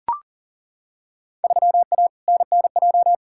Morse code, heard as a keyed tone, about 40 wpm, 700 Hz. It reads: 3A NNJ
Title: Field Day Exchanges Encore - 40wpm